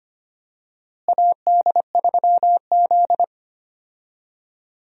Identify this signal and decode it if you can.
Morse code, heard as a keyed tone, about 25 wpm, 700 Hz. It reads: AD3Z
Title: USA Call Signs - 4 Characters - Repeated 2x - 25wpm